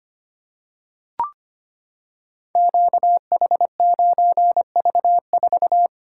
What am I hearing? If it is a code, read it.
Morse code, heard as a keyed tone, about 25 wpm, 700 Hz. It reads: QH9V4
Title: Code Groups - 5 Letter:Numbers - 25wpm